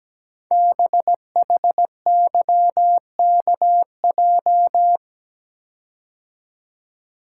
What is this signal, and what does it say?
Morse code, heard as a keyed tone, about 17 wpm, 700 Hz. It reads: BHYKJ
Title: Code Groups - 5 Letters - 17wpm